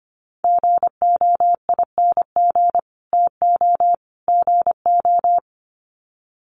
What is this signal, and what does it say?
Morse code, heard as a keyed tone, about 25 wpm, 700 Hz. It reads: GOING TO GO
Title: Sets of 3 Words 25wpm